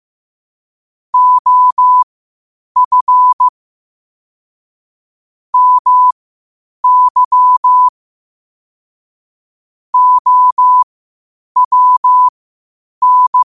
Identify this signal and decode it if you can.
Morse code, heard as a keyed tone, about 15 wpm, spaced as for 8 wpm, 1000 Hz. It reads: OF MY OWN
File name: qotdfiles / 8wpm